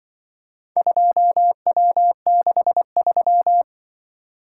Morse code, heard as a keyed tone, about 24 wpm, 700 Hz. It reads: 2W63
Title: Speed-Racing - Four Letter-Number Combinations - 20wpm